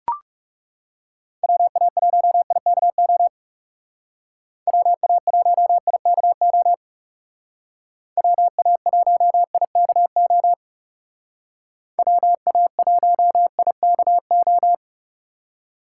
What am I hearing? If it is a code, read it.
Morse code, heard as a keyed tone, about 36 wpm, 700 Hz. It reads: WA1IKO WA1IKO WA1IKO WA1IKO
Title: Speed Racing - USA 2x3 Call Signs 30wpm